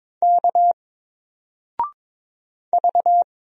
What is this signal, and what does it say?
Morse code, heard as a keyed tone, about 22 wpm, 700 Hz. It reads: K V
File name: Single-Letter-Number-22wpm